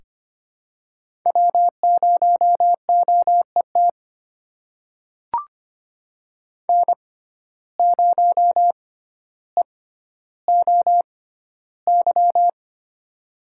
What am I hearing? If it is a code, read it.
Morse code, heard as a keyed tone, about 25 wpm, spaced as for 13 wpm, 700 Hz. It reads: W0OET N0EOY
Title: USA Call Signs - 5 Characters - 6x Character Spacing - 25wpm